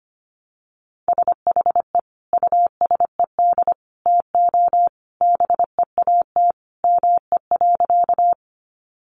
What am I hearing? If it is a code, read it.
Morse code, heard as a keyed tone, about 25 wpm, 700 Hz. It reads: SHE USED TO BEAT ME.
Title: Sentences from Top 1300 Words 25wpm